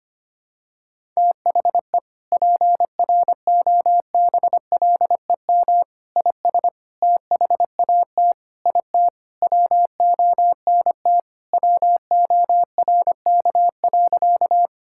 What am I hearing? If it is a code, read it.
Morse code, heard as a keyed tone, about 25 wpm, 700 Hz. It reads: THE PROBLEM IS THAT IT WONT WORK.
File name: practice / 25wpm / Sentences with 3-Word Phrases - 25wpm